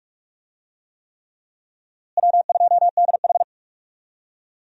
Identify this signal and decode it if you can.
Morse code, heard as a keyed tone, about 46 wpm, 700 Hz. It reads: W2DH
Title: USA Call Signs - 4 Characters - Repeated 3x - 45wpm